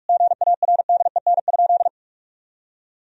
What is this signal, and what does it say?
Morse code, heard as a keyed tone, about 46 wpm, 700 Hz. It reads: GARDEN?
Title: Sentences from Top 1000 Words - Review Part 4 - 45wpm